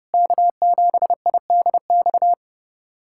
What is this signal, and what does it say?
Morse code, heard as a keyed tone, about 32 wpm, 700 Hz. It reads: K7IDX
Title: USA Callsign DE Callsign 30wpm